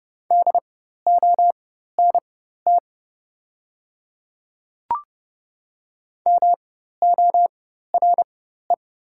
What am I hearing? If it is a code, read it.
Morse code, heard as a keyed tone, about 30 wpm, spaced as for 14 wpm, 700 Hz. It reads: DONT MORE
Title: 4x Character Spacing - Top 300 Words - 30wpm